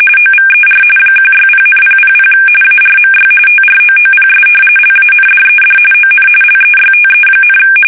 RTTY, about 45.45 baud. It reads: OVER 160,000 ARMY TROOPS LANDED ON LEYTE,
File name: fsk850 / battleshipnj